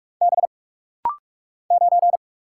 Morse code, heard as a keyed tone, about 44 wpm, 700 Hz. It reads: B 9
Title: Single Letter-Number - Rapid-Fire 45wpm